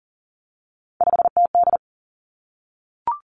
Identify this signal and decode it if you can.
Morse code, heard as a keyed tone, about 38 wpm, 700 Hz. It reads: 5TD
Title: Three Challenging Letter-Number Combinations 40wpm